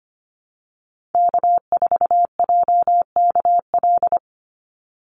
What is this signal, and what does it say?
Morse code, heard as a keyed tone, about 25 wpm, 700 Hz. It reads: K4JKL
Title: Call Signs Repeated 2x - 25wpm